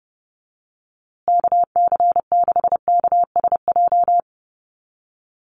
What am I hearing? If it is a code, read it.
Morse code, heard as a keyed tone, about 30 wpm, 700 Hz. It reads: KC6KSJ
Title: USA Call Signs Repeated 3x 30wpm